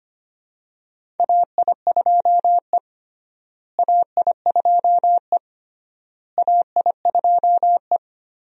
Morse code, heard as a keyed tone, about 25 wpm, 700 Hz. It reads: AI2E AI2E AI2E
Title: USA Call Signs - 4 Characters - Repeated 3x - 25wpm